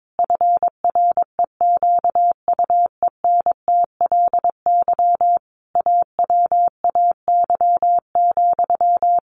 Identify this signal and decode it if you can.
Morse code, heard as a keyed tone, about 22 wpm, 700 Hz. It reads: FREQUENTLY AWAY,
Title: Sentences from Top 1400 Words 22wpm